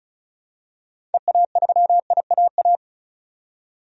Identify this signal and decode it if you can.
Morse code, heard as a keyed tone, about 34 wpm, 700 Hz. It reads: EA3IAA
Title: DXCC-100 Call Signs - 6 Characters - 35wpm